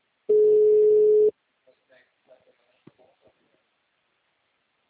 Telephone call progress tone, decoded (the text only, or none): ringback tone